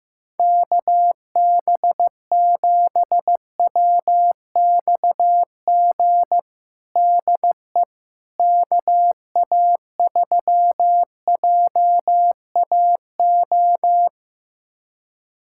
Morse code, heard as a keyed tone, about 15 wpm, 700 Hz. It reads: KB7WXG DE KA3JAO
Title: USA Callsign DE Callsign 15wpm